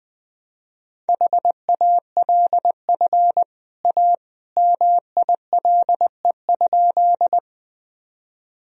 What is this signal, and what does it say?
Morse code, heard as a keyed tone, about 20 wpm, 700 Hz. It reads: HALF A MILE?
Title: Sentences from Top 1000 Words - Review Part 1 - 20wpm